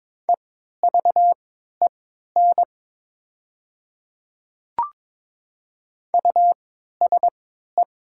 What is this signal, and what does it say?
Morse code, heard as a keyed tone, about 22 wpm, spaced as for 12 wpm, 700 Hz. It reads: EVEN USE
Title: 3x Character Spacing - Top 500 Words - 22wpm